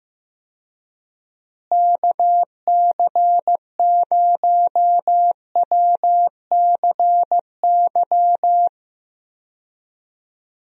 Morse code, heard as a keyed tone, about 15 wpm, 700 Hz. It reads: KC0WCY
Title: USA Call Signs - 6 Characters - Repeated 3x - 15wpm